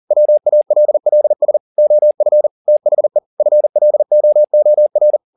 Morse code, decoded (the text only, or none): WALLS OF THE FLOOR